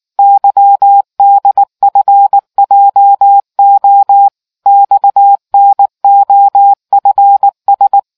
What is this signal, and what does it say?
Morse code, heard as a keyed tone, about 19 wpm, 800 Hz. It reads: YDFJO XNOFS